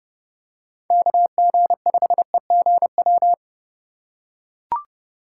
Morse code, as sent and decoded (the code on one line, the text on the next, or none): -.- --. ..... --. .--
KG5EGW